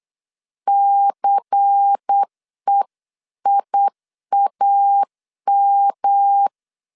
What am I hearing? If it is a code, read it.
Morse code, heard as a keyed tone, about 8 wpm, 800 Hz. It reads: CEIAM
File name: CW / DET / 1 6 1500C S B 8wpm 5142